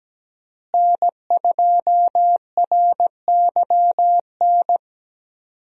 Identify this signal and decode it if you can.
Morse code, heard as a keyed tone, about 17 wpm, 700 Hz. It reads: N2RYN
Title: USA Call Signs - 5 Characters - Repeated 3x - 17wpm